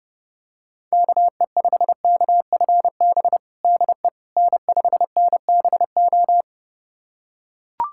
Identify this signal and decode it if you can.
Morse code, heard as a keyed tone, about 30 wpm, 700 Hz. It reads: KE5KFB DE N5NBO
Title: USA Callsign DE Callsign 30wpm